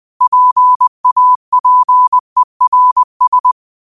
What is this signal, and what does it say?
Morse code, heard as a keyed tone, about 20 wpm, 1000 Hz. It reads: PAPERS
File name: qotdfiles / 20wpm